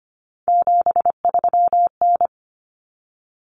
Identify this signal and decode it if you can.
Morse code, heard as a keyed tone, about 25 wpm, 700 Hz. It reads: 73N
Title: Common Cut Numbers - 3-Digit 25wpm